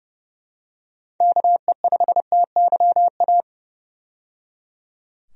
Morse code, heard as a keyed tone, about 30 wpm, 700 Hz. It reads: KE5TYA
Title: USA Call Signs - 6 Characters - 30wpm